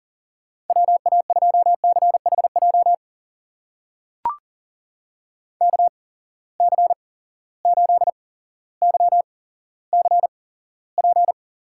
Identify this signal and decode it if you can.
Morse code, heard as a keyed tone, about 38 wpm, spaced as for 17 wpm, 700 Hz. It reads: WA2CHJ KC8YCP
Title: USA Call Signs - 6 Characters - 8x Character Spacing - 40wpm